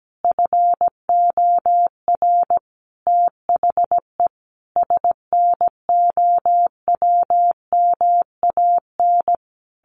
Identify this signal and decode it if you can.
Morse code, heard as a keyed tone, about 17 wpm, 700 Hz. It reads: FOR THE SNOWMAN
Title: Easy Phrases 17wpm